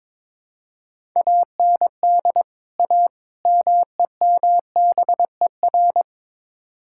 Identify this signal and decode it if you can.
Morse code, heard as a keyed tone, about 21 wpm, 700 Hz. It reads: AND A MEMBER
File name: practice / 22wpm / Build A Phrase - 5-Word Phrases - 22wpm